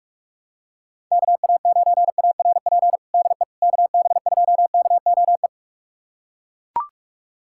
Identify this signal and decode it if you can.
Morse code, heard as a keyed tone, about 41 wpm, 700 Hz. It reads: KA9ARP DE KB2KOE
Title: USA Callsign DE Callsign - 6 Characters - 45wpm